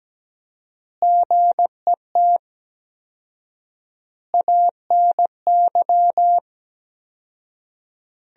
Morse code, heard as a keyed tone, about 17 wpm, 700 Hz. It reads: GET ANY
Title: 4x Word-Spacing - Sentences from Top 400 Words - 17wpm